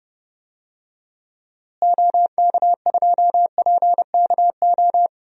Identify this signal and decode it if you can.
Morse code, heard as a keyed tone, about 32 wpm, 700 Hz. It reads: OK2PKO
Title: DXCC-100 Call Signs - 6 Characters - Repeated 2x - 30wpm